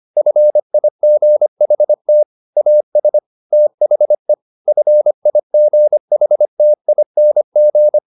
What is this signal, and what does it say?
Morse code, heard as a keyed tone, about 25 wpm, 600 Hz. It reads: FIGHT AS THE FIGHTING
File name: the old man and the sea part 1 25wpm